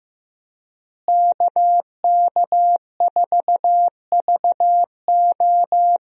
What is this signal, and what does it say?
Morse code, heard as a keyed tone, about 15 wpm, 700 Hz. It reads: KK4VO
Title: USA Call Signs - 5 Characters - Repeated 2x - 15wpm